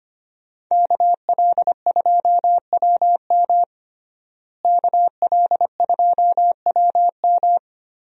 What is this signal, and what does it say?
Morse code, heard as a keyed tone, about 25 wpm, 700 Hz. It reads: KL2WM KL2WM
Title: USA Call Signs - 5 Characters - Repeated 3x - 25wpm